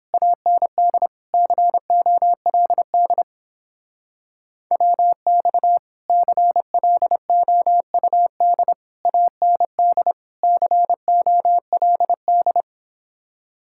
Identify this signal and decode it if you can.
Morse code, heard as a keyed tone, about 31 wpm, 700 Hz. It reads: AND COLD WX CLOUD AND COLD